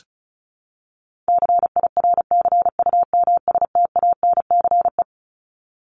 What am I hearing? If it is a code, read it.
Morse code, heard as a keyed tone, about 34 wpm, 700 Hz. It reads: CIRCUMSTANCE
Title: ES:S Suffix - Top 100 Words with Base Words 35wpm